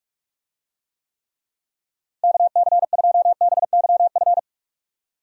Morse code, heard as a keyed tone, about 46 wpm, 700 Hz. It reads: KC2BYF
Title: USA Call Signs - 6 Characters - Repeated 3x - 45wpm